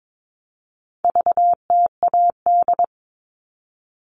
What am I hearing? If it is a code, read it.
Morse code, heard as a keyed tone, about 22 wpm, 700 Hz. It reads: VTAD
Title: All Cut Numbers - 4-Digit 22wpm